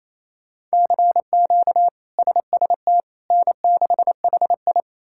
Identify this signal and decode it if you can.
Morse code, heard as a keyed tone, about 28 wpm, 700 Hz. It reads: CQ SST N6HI
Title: K1USN SST QSO Practice - 28wpm